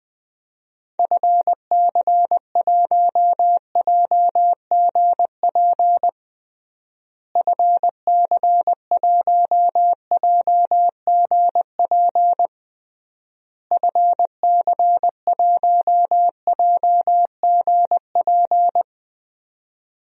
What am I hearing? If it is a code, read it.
Morse code, heard as a keyed tone, about 21 wpm, 700 Hz. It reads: FC1JGP FC1JGP FC1JGP
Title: DXCC-100 Call Signs - 6 Characters - Repeated 3x - 20wpm